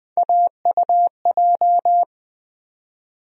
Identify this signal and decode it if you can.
Morse code, heard as a keyed tone, about 20 wpm, 700 Hz. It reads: AUJ
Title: USA Call Sign Postfixes - 3 Letters 20wpm